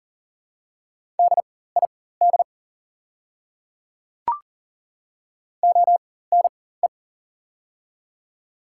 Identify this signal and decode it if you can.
Morse code, heard as a keyed tone, about 42 wpm, spaced as for 19 wpm, 700 Hz. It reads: DID ONE
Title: 4x Character Spacing - Top 100 Words - 40wpm